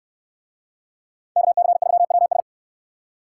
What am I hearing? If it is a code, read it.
Morse code, heard as a keyed tone, about 68 wpm, 700 Hz. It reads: KC3WS